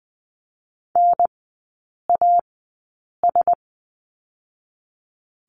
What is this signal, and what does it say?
Morse code, heard as a keyed tone, about 20 wpm, spaced as for 8 wpm, 700 Hz. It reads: NAS
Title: 081 Lesson 14e - Number 2 - Three Characters 20wpm